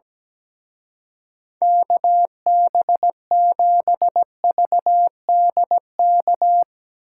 Morse code, heard as a keyed tone, about 17 wpm, 700 Hz. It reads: KB7VDK
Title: Speed Racing - USA 2x3 Call Signs 17wpm